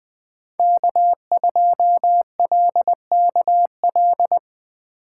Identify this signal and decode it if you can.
Morse code, heard as a keyed tone, about 23 wpm, 700 Hz. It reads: K2LKL
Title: Speed Racing - USA 1x3 Call Signs 20wpm